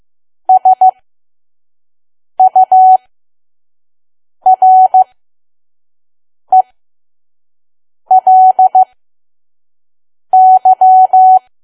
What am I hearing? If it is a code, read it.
Morse code, heard as a keyed tone, about 15 wpm, spaced as for 5 wpm, 750 Hz. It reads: SURELY
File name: Archive / 5 WPM files / PM